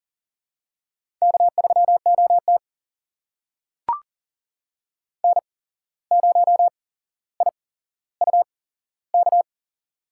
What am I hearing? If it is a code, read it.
Morse code, heard as a keyed tone, about 40 wpm, spaced as for 16 wpm, 700 Hz. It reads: K3OT N0IUK
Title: USA Call Signs - 8x Character Spacing - 40wpm